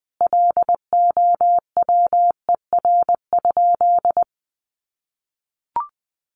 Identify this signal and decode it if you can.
Morse code, heard as a keyed tone, about 20 wpm, 700 Hz. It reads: LOWER?
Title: Sentences from Top 700 Words 20wpm